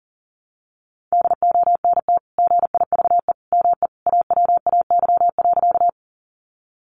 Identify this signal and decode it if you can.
Morse code, heard as a keyed tone, about 37 wpm, 700 Hz. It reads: DONT GIVE ME AWAY.
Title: Sentences from Top 600 Words 40wpm